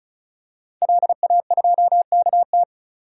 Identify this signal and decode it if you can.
Morse code, heard as a keyed tone, about 36 wpm, 700 Hz. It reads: LA2KT
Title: DXCC-100 Call Signs - 5 Characters - 35wpm